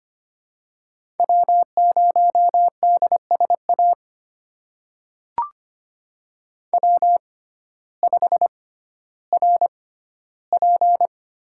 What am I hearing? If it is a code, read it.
Morse code, heard as a keyed tone, about 25 wpm, spaced as for 14 wpm, 700 Hz. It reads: W0DSA W5RP
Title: USA Call Signs - 5 Characters - 6x Character Spacing - 25wpm